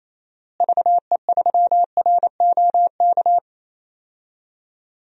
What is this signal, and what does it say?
Morse code, heard as a keyed tone, about 27 wpm, 700 Hz. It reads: VE3ROK
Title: DXCC-100 Call Signs - 6 Characters - Repeated 2x - 28wpm